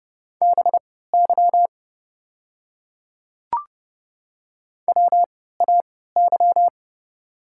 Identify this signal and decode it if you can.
Morse code, heard as a keyed tone, about 30 wpm, spaced as for 17 wpm, 700 Hz. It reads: BY WAY
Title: 3x Character Spacing - Top 200 Words - 30wpm